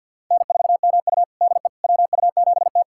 Morse code, heard as a keyed tone, about 52 wpm, 700 Hz. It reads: N4MU DE WU7T
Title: USA Callsign DE Callsign - 4 Characters - 50wpm